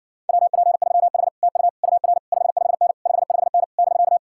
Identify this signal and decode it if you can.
Morse code, heard as a keyed tone, about 63 wpm, 700 Hz. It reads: WP3H TU UR 55N 55N <BK>